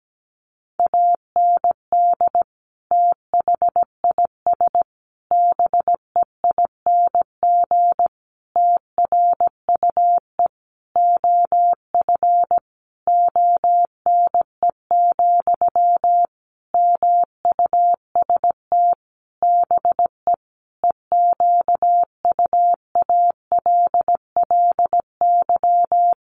Morse code, heard as a keyed tone, about 17 wpm, 700 Hz. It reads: AND THIS BEING TRUE OF ONE, MUST BE EQUALLY
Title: Sentences from Top 1400 Words 17wpm